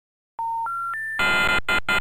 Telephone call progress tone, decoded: special information tone